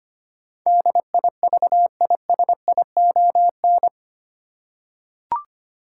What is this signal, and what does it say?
Morse code, heard as a keyed tone, about 25 wpm, 700 Hz. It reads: DIVISION
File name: 8 Letter Words - First 3 Letters Given - 25wpm